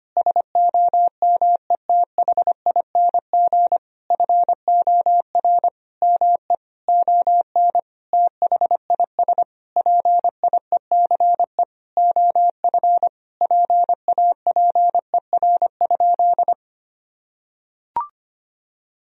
Morse code, heard as a keyed tone, about 25 wpm, 700 Hz. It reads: SOMETHING FOR ME ON THIS PIECE OF PAPER?
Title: Sentences from Top 1000 Words - Review Part 2 - 25wpm